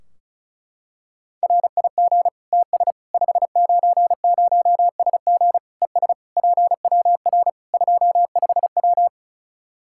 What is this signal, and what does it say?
Morse code, heard as a keyed tone, about 34 wpm, 700 Hz. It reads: RIG TS 590SG ES PWR 25W
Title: Practice QSO - Exchange 2 - Info - 35wpm